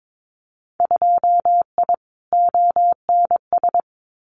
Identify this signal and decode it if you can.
Morse code, heard as a keyed tone, about 22 wpm, 700 Hz. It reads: 2I ONS
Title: Winter Field Day Exchanges Encore 22wpm